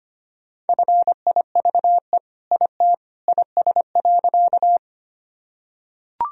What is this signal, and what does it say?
Morse code, heard as a keyed tone, about 25 wpm, 700 Hz. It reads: FIVE IT IS.
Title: Sentences from Top 1000 Words - Review Part 2 - 25wpm